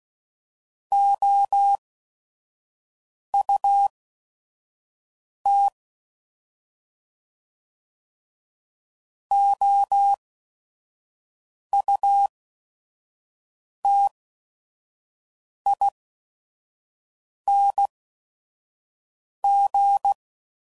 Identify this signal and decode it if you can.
Morse code, heard as a keyed tone, about 16 wpm, spaced as for 5 wpm, 800 Hz. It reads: OUT OUTING
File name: Prefixes / By First Letter / Os